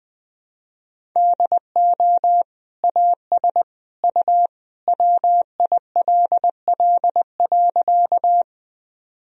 Morse code, heard as a keyed tone, about 20 wpm, 700 Hz. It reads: DO AS U WILL.
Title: Sentences from Top 100 Words and Abbreviations 20wpm